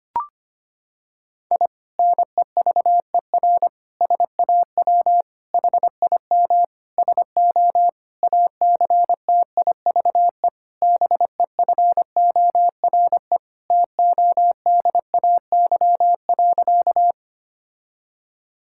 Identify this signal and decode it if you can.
Morse code, heard as a keyed tone, about 25 wpm, 700 Hz. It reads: I NEVER SAW HIM SO ACTIVE BEFORE TODAY.
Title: Sentences from Top 1000 Words - Review Part 6 - 25wpm